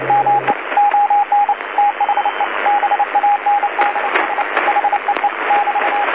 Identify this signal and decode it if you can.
Morse code, heard as a keyed tone, about 30 wpm, 820 Hz, with radio noise. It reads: ME ON THE BANDS HEE <BT>
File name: DC-TRCVR / 80-CW